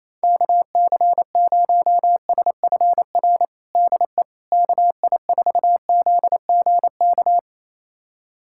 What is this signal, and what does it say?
Morse code, heard as a keyed tone, about 28 wpm, 700 Hz. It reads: KC0SFR DE KI4ZGK
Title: USA Callsign DE Callsign - 6 Characters - 28wpm